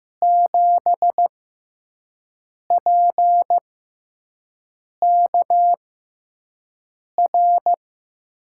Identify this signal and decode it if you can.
Morse code, heard as a keyed tone, about 15 wpm, spaced as for 5 wpm, 700 Hz. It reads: 7PKR
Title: USA Call Signs - 6 Characters - 6x Character Spacing - 15wpm